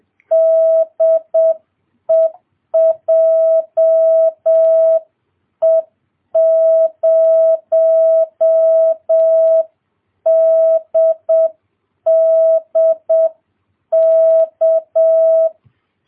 Morse code, heard as a keyed tone, about 7 wpm, 650 Hz. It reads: DE JE0DDK